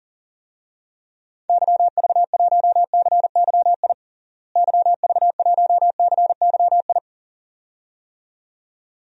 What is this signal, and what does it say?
Morse code, heard as a keyed tone, about 40 wpm, 700 Hz. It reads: YV1CYI YV1CYI
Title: DXCC-100 Call Signs - 6 Characters - Repeated 2x - 40wpm